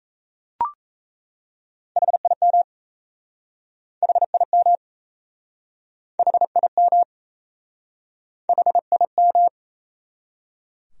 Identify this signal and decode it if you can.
Morse code, heard as a keyed tone, about 38 wpm, 700 Hz. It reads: HIM HIM HIM HIM